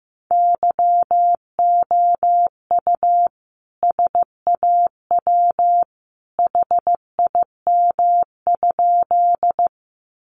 Morse code, heard as a keyed tone, about 15 wpm, 700 Hz. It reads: YOU SAW HIM?
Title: Sentences from Top 300 Words 15wpm